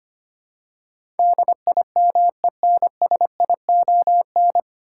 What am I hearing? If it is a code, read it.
Morse code, heard as a keyed tone, about 25 wpm, 690 Hz. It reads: DIMENSION
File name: practice / 25wpm / 9 Letter Words - First 3 Letters Given - 25wpm